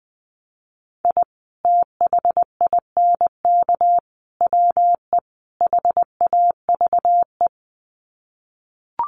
Morse code, heard as a keyed tone, about 19 wpm, 700 Hz. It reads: I THINK WE HAVE
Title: Sets of 4 Words 20wpm